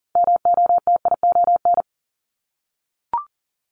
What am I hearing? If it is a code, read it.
Morse code, heard as a keyed tone, about 38 wpm, 700 Hz. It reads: MOTION
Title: Multi-letter Phonemes 40wpm